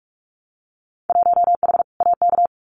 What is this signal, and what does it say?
Morse code, heard as a keyed tone, about 46 wpm, 700 Hz. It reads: 1H AK
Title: Winter Field Day Exchanges Encore 45wpm